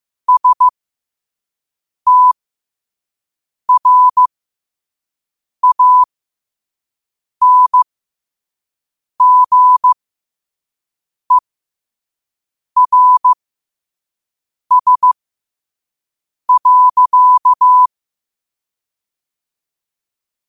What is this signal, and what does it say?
Morse code, heard as a keyed tone, about 15 wpm, spaced as for 5 wpm, 1000 Hz. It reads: STRANGERS.
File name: qotdfiles / 5wpm